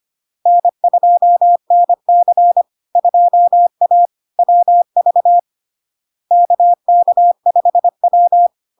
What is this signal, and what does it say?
Morse code, heard as a keyed tone, about 25 wpm, 700 Hz. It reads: N2NC 2A WV KK5W
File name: LARK-FD-2022-Log-in-CW-25wpm-700-Hz